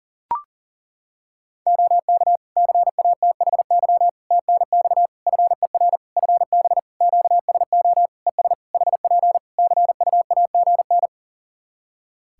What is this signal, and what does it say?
Morse code, heard as a keyed tone, about 40 wpm, 700 Hz. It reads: OK CATHY TNX FER FB QSO ES HP CUAGN
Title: K1USN SST QSO Practice 40wpm